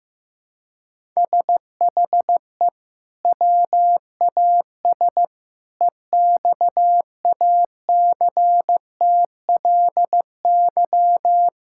Morse code, heard as a keyed tone, about 15 wpm, 700 Hz. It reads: SHE WAS EXACTLY
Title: Sentences from Top 1000 Words - Review Part 6 - 15wpm